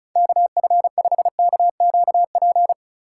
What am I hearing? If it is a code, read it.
Morse code, heard as a keyed tone, about 36 wpm, 700 Hz. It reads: KF5KQP
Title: USA Call Signs - 2x Character Spacing - 35wpm